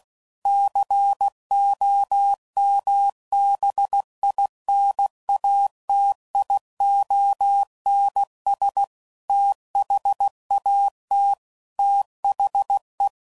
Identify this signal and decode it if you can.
Morse code, heard as a keyed tone, about 16 wpm, 800 Hz. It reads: COMBINATIONS THAT THE
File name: CW Code Groups @15WPM / Technical Subjects / AND Gate